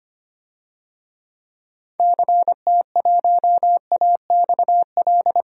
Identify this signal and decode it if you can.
Morse code, heard as a keyed tone, about 25 wpm, 700 Hz. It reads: CT1AXL
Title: DXCC-100 Call Signs - 6 Characters - Repeated 3x - 25wpm